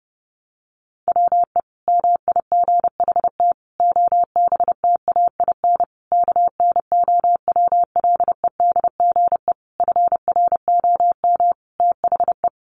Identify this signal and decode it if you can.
Morse code, heard as a keyed tone, about 30 wpm, 700 Hz. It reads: WE MIGHT OBTAIN KNOWLEDGE FROM THE
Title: Sentences from Top 1300 Words 30wpm